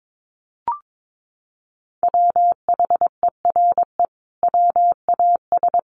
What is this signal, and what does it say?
Morse code, heard as a keyed tone, about 22 wpm, 700 Hz. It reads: WHERE WAS
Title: Sentences from Top 1500 Words 22wpm